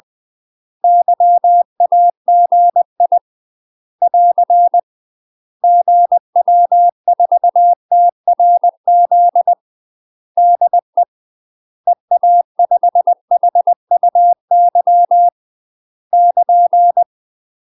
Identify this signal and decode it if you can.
Morse code, heard as a keyed tone, about 20 wpm, 700 Hz. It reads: YAGI <AR> GW4TRZ DE EA5HUY <KN>
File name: UK MASTER QSOs TRAINING / 20WPM / UK MASTER QSOs 0043